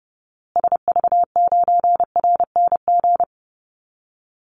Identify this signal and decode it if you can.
Morse code, heard as a keyed tone, about 30 wpm, 700 Hz. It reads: SV9RNG
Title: Call Signs Repeated 3x - 30wpm